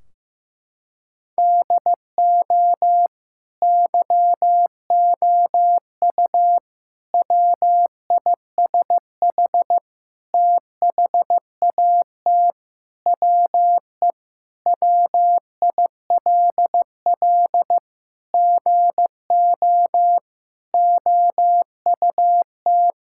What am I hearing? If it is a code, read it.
Morse code, heard as a keyed tone, about 15 wpm, 700 Hz. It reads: DO YOU WISH THAT WE WILL GO OUT
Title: Sentences from Top 1600 Words 15wpm